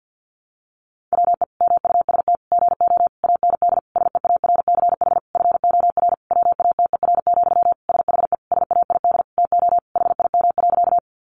Morse code, heard as a keyed tone, about 50 wpm, 700 Hz. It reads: WE MUST GO AND SEARCH FOR WATER, SHE SAID TO HIM.
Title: Sentences from Top 800 Words 50wpm